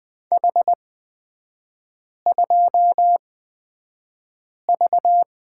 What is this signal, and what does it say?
Morse code, heard as a keyed tone, about 20 wpm, spaced as for 5 wpm, 700 Hz. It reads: H2V
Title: 224 Lesson 36e - BT - Three Characters 20wpm